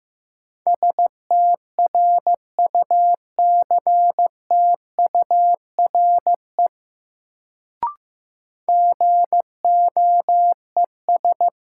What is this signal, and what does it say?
Morse code, heard as a keyed tone, about 15 wpm, 700 Hz. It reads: STRUCTURE GOES
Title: Sentences from Top 1500 Words - New Word Drill Alt - 15wpm